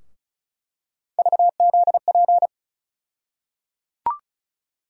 Morse code, heard as a keyed tone, about 36 wpm, 700 Hz. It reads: VZP
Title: USA Call Sign Postfixes - 3 Letters 35wpm